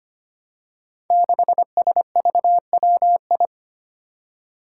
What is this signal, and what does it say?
Morse code, heard as a keyed tone, about 25 wpm, 700 Hz. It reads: 6SVWI
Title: Five Challenging Letter-Number Combinations 25wpm